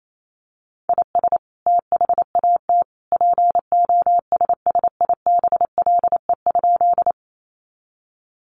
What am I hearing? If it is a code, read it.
Morse code, heard as a keyed tone, about 28 wpm, 700 Hz. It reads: IS THAT POSSIBLE?
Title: Sentences from Top 500 Words 28wpm